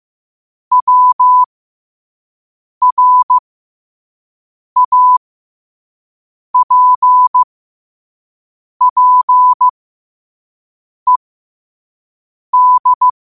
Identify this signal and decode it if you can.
Morse code, heard as a keyed tone, about 15 wpm, spaced as for 5 wpm, 1000 Hz. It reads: WRAPPED
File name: qotdfiles / 5wpm